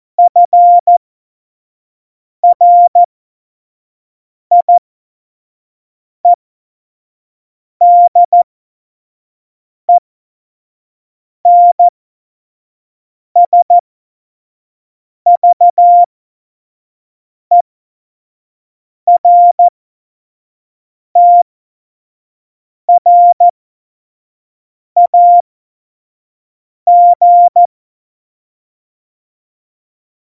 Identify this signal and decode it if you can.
Morse code, heard as a keyed tone, about 14 wpm, spaced as for 5 wpm, 700 Hz. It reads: FRIEDENSVERTRAG
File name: Morsekurs / Witze / Witz 16. Juli 2019, 25 F70